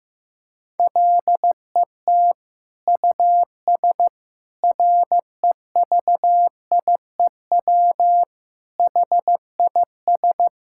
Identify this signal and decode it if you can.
Morse code, heard as a keyed tone, about 15 wpm, 700 Hz. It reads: LET US REVIEW HIS
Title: Sentences from Top 1000 Words - Review Part 6 - 15wpm